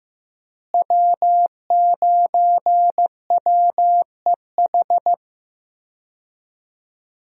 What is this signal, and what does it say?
Morse code, heard as a keyed tone, about 15 wpm, 700 Hz. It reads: W9WEH
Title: USA Call Signs - 5 Characters - Repeated 2x - 15wpm